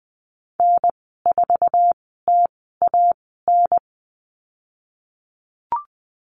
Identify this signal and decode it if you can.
Morse code, heard as a keyed tone, about 19 wpm, spaced as for 14 wpm, 700 Hz. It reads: N4TAN
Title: 055 Lesson 10g - Letter r - Call Signs 20wpm